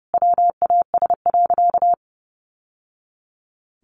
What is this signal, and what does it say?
Morse code, heard as a keyed tone, about 30 wpm, 700 Hz. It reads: WAS.
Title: Sentences from Top 600 Words 30wpm